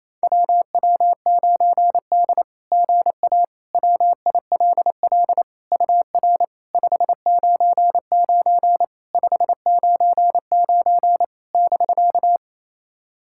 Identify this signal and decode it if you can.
Morse code, heard as a keyed tone, about 28 wpm, 690 Hz. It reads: WW9D GA WILL UR 599 599 <BK>
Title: SOTA QSO Practice 28wpm